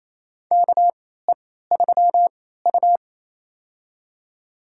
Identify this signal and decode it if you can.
Morse code, heard as a keyed tone, about 28 wpm, spaced as for 16 wpm, 700 Hz. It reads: KE3U
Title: USA Call Signs - 3x Character Spacing - 28wpm